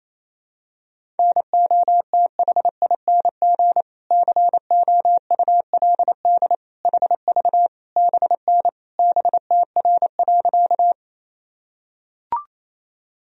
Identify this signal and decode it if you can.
Morse code, heard as a keyed tone, about 28 wpm, 700 Hz. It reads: NOTHING COULD HV BN BTR.
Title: Sentences from Top 500 Words and Abbreviations 28wpm